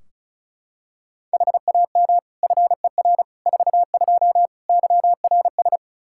Morse code, heard as a keyed tone, about 34 wpm, 700 Hz. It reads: HAM FER 42 YRS
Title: Practice QSO - Exchange 3 - More Info - 35wpm